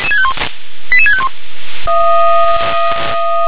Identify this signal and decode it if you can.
Five-tone selective call (ZVEI), heard as quick runs of five tones, tones of about 70 ms. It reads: 88651
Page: ZVEI